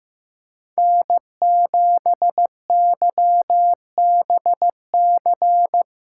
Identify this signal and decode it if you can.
Morse code, heard as a keyed tone, about 15 wpm, 700 Hz. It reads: N7YBC
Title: USA Call Signs - 5 Characters - 4x Character Spacing - 15wpm